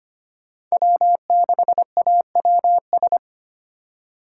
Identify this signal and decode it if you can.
Morse code, heard as a keyed tone, about 25 wpm, 700 Hz. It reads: W6AWS
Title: K1USN SST QSO Practice - 25wpm